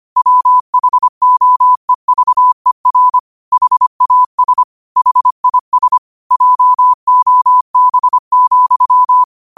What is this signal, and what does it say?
Morse code, heard as a keyed tone, about 25 wpm, 1000 Hz. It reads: WHOEVER HAS HIS JOB,
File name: qotdfiles / 25wpm